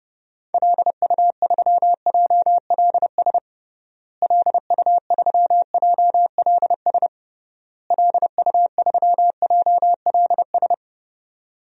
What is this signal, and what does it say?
Morse code, heard as a keyed tone, about 32 wpm, 700 Hz. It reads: LU3JLS LU3JLS LU3JLS
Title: DXCC-100 Call Signs - 6 Characters - Repeated 3x - 30wpm